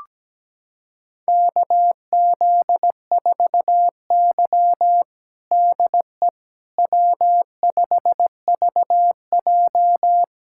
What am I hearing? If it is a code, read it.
Morse code, heard as a keyed tone, about 17 wpm, 700 Hz. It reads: KZ4Y DE W5VJ
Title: USA Callsign DE Callsign - 4 Characters - 17wpm